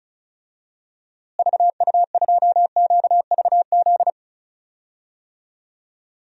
Morse code, heard as a keyed tone, about 34 wpm, 700 Hz. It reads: VU2QVZ
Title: DXCC-100 Call Signs - 6 Characters - Repeated 3x - 35wpm